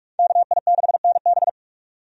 Morse code, heard as a keyed tone, about 41 wpm, 700 Hz. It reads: KI6NB
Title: USA Call Signs - 5 Characters - 45wpm